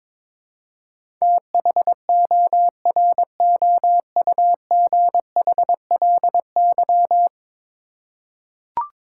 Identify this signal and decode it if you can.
Morse code, heard as a keyed tone, about 22 wpm, 700 Hz. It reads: THOROUGHLY
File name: practice / 22wpm / 10 Letter Words - First 3 Letters Given - 22wpm